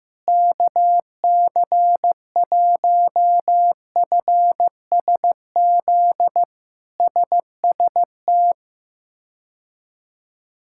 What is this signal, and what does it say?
Morse code, heard as a keyed tone, about 16 wpm, 700 Hz. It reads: KC1FSZ SST
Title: K1USN SST QSO Practice - 15wpm